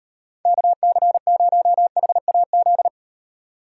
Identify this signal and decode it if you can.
Morse code, heard as a keyed tone, about 38 wpm, 700 Hz. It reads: KC0HAZ